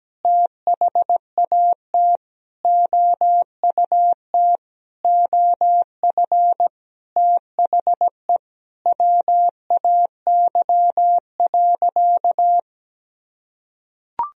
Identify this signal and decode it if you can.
Morse code, heard as a keyed tone, about 17 wpm, 700 Hz. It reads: THAT OUT OF THE WAY.
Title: Sentences from Top 1000 Words - Review Part 4 - 17wpm